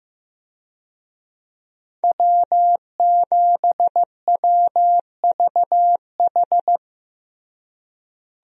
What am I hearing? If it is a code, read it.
Morse code, heard as a keyed tone, about 15 wpm, 700 Hz. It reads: W7WVH
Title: USA Call Signs - 5 Characters - Repeated 2x - 15wpm